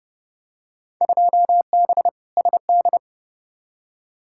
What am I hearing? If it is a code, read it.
Morse code, heard as a keyed tone, about 32 wpm, 700 Hz. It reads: 2B SD
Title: Field Day Exchanges Encore - 30wpm